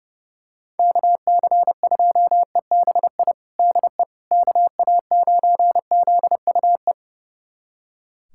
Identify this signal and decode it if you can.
Morse code, heard as a keyed tone, about 30 wpm, 700 Hz. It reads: KC2EBI DE KA9ZUE
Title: USA Callsign DE Callsign 30wpm